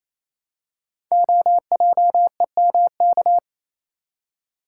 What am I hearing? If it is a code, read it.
Morse code, heard as a keyed tone, about 27 wpm, 700 Hz. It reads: OJEMK
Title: Code Groups - 5 Letters - 28wpm